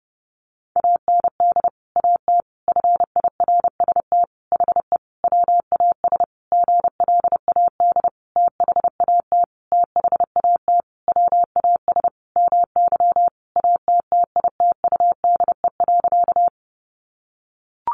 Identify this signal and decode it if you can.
Morse code, heard as a keyed tone, about 30 wpm, 700 Hz. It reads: AND AT FIRST HE WAS GLAD THAT THAT WAS MY ATTITUDE.
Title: Sentences from Top 1300 Words 30wpm